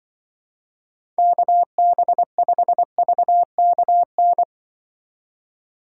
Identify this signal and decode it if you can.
Morse code, heard as a keyed tone, about 24 wpm, 700 Hz. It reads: KB5VKN
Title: Speed Racing - USA Call Signs 20wpm